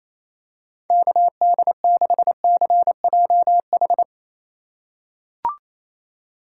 Morse code, heard as a keyed tone, about 27 wpm, 700 Hz. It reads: KD6CJH
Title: USA Call Signs - 6 Characters - 2x Character Spacing - 28wpm